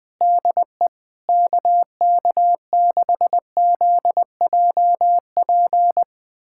Morse code, heard as a keyed tone, about 20 wpm, 700 Hz. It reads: DE KK6ZJP
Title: USA Callsign DE Callsign - 6 Characters - 20wpm